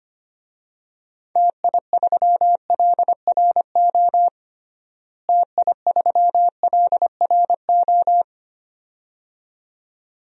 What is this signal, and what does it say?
Morse code, heard as a keyed tone, about 25 wpm, 700 Hz. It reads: TI3LRO TI3LRO
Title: DXCC-100 Call Signs - 6 Characters - Repeated 2x - 25wpm